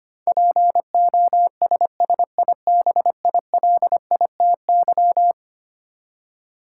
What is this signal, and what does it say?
Morse code, heard as a keyed tone, about 25 wpm, 700 Hz. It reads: POSSIBILITY
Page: morse-code-ninja-practice-25wpm